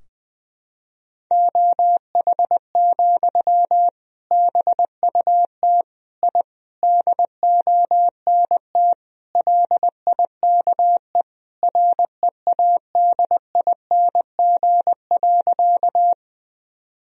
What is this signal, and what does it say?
Morse code, heard as a keyed tone, about 20 wpm, 700 Hz. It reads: OH, BUT I DONT LIKE READING.
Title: Sentences from Top 1000 Words - Review Part 3 - 20wpm